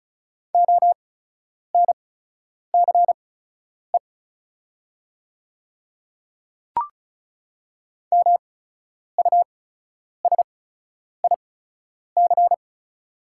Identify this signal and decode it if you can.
Morse code, heard as a keyed tone, about 36 wpm, spaced as for 10 wpm, 690 Hz. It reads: ONCE MUSIC
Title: 8x Character Spacing - Top 300 Words - 35wpm